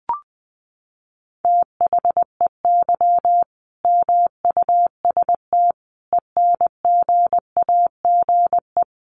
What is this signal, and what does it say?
Morse code, heard as a keyed tone, about 20 wpm, 700 Hz. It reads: THEY MUST ENGAGE
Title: Sentences from Top 1500 Words 20wpm